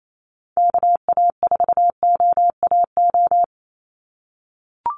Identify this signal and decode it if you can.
Morse code, heard as a keyed tone, about 28 wpm, 700 Hz. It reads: KA4OAO
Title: USA Call Signs 28wpm